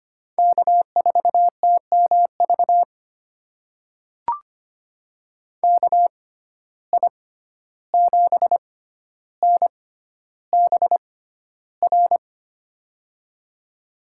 Morse code, heard as a keyed tone, about 25 wpm, spaced as for 12 wpm, 700 Hz. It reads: K4TMV KI7NBR
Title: USA Call Signs - 6x Character Spacing - 25wpm